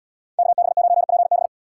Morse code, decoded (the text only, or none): KB9PB